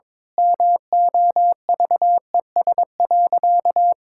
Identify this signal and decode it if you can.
Morse code, heard as a keyed tone, about 22 wpm, 700 Hz. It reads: MOVES.
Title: Pangram Challenge v2 - 22wpm